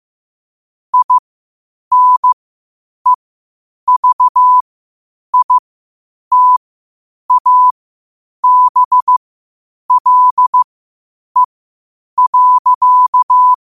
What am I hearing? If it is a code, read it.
Morse code, heard as a keyed tone, about 15 wpm, spaced as for 8 wpm, 1000 Hz. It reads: INEVITABLE.
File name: qotdfiles / 8wpm